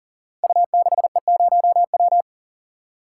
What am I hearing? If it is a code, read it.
Morse code, heard as a keyed tone, about 41 wpm, 700 Hz. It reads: U6E0W
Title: Code Groups - 5 Letter:Numbers - 40wpm